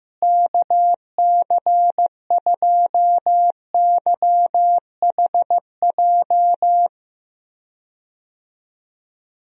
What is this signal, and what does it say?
Morse code, heard as a keyed tone, about 15 wpm, 700 Hz. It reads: KC2YHJ
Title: USA Call Signs - 6 Characters - Repeated 3x - 15wpm